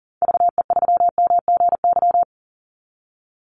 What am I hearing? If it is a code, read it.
Morse code, heard as a keyed tone, about 40 wpm, 700 Hz. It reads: VE3MGY
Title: Call Signs Repeated 3x - 40wpm